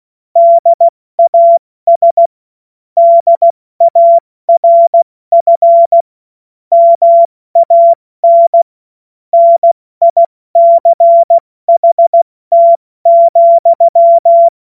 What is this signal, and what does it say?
Morse code, heard as a keyed tone, about 16 wpm, 680 Hz. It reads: DAS DARF MAN NICHT,
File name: Morsekurs / Witze / Witz 23. August 2022 70 F80